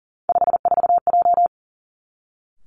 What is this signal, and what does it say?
Morse code, heard as a keyed tone, about 38 wpm, 700 Hz. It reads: 54J
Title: Three Challenging Letter-Number Combinations 40wpm